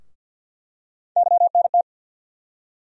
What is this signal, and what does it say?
Morse code, heard as a keyed tone, about 48 wpm, 700 Hz. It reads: YNT
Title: USA Call Sign Postfixes - 3 Letters 50wpm